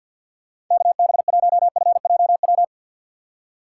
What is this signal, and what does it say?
Morse code, heard as a keyed tone, about 52 wpm, 700 Hz. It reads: KB1FJW
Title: USA Call Signs - 6 Characters - 4x Character Spacing - 50wpm